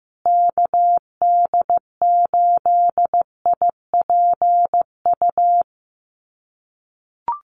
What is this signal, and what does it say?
Morse code, heard as a keyed tone, about 15 wpm, 700 Hz. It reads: KD8IPU
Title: USA Call Signs Repeated 2x 15wpm